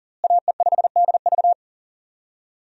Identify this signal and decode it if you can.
Morse code, heard as a keyed tone, about 40 wpm, 700 Hz. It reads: AE5DV
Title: USA Call Signs - 5 Characters - Repeated 2x - 40wpm